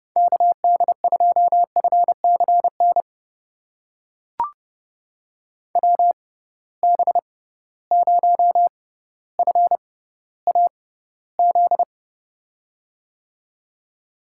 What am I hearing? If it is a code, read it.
Morse code, heard as a keyed tone, about 29 wpm, spaced as for 15 wpm, 700 Hz. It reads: KD2FCN WB0FAZ
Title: USA Call Signs - 6 Characters - 6x Character Spacing - 30wpm